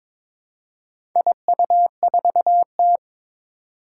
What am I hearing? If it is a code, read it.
Morse code, heard as a keyed tone, about 22 wpm, 700 Hz. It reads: IU4T
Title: Four Challenging Letter-Number Combinations 22wpm